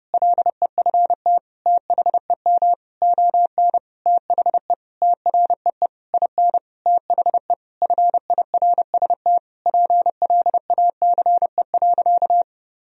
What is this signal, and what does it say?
Morse code, heard as a keyed tone, about 30 wpm, 700 Hz. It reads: LEFT THEM ON THE TREE IN THE FIRST PLACE.
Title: Sentences from Top 1000 Words - Review Part 1 - 30wpm